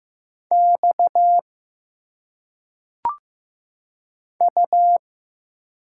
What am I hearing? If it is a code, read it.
Morse code, heard as a keyed tone, about 15 wpm, 700 Hz. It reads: X U
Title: Single Letters - Phonetic Pronunciation 15wpm